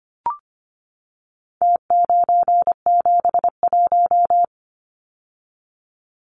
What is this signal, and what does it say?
Morse code, heard as a keyed tone, about 25 wpm, 700 Hz. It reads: T971
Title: Numbers - Contest Style - 4-Digit 25wpm